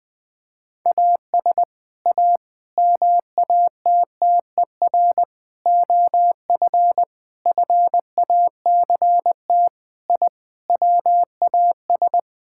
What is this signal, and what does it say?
Morse code, heard as a keyed tone, about 20 wpm, 700 Hz. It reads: AS A MATTER OF FACT I WAS
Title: Sentences with 5-Word Phrases - 20wpm